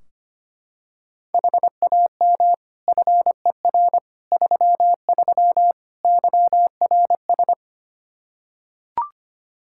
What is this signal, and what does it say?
Morse code, heard as a keyed tone, about 25 wpm, 700 Hz. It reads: HAM FER 33 YRS
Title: Practice QSO - Exchange 3 - More Info - 25wpm